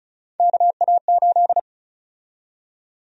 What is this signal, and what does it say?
Morse code, heard as a keyed tone, about 34 wpm, 700 Hz. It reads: KA8
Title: USA Call Sign Prefixes - 35wpm